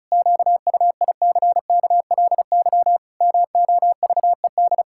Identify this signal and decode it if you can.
Morse code, heard as a keyed tone, about 34 wpm, 700 Hz. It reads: QUICKLY MOVED
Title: Pangram Challenge v2 - 35wpm